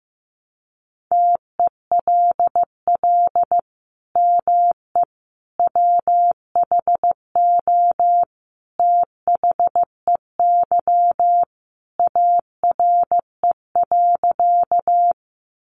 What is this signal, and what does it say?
Morse code, heard as a keyed tone, about 15 wpm, 700 Hz. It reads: TELL ME WHO THEY ARE.
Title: Sentences from Top 700 Words 15wpm